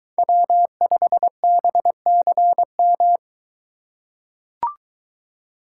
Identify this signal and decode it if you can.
Morse code, heard as a keyed tone, about 22 wpm, 700 Hz. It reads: W5BCM